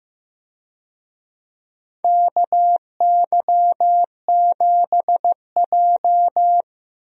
Morse code, heard as a keyed tone, about 15 wpm, 700 Hz. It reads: KY7J
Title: USA Call Signs - 4 Characters - Repeated 3x - 15wpm